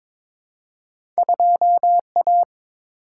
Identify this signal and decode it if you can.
Morse code, heard as a keyed tone, about 22 wpm, 700 Hz. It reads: 2A
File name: practice / 22wpm / Two Letter-Number Combinations - 22wpm